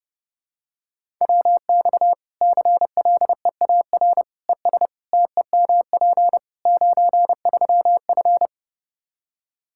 Morse code, heard as a keyed tone, about 30 wpm, 700 Hz. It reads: WX CLEAR ES TEMP 93F
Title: K1USN SST QSO Practice 30wpm